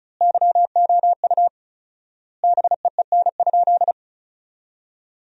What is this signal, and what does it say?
Morse code, heard as a keyed tone, about 36 wpm, 700 Hz. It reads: YOU BEEN?
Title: 4x Word-Spacing - Sentences from Top 400 Words - 35wpm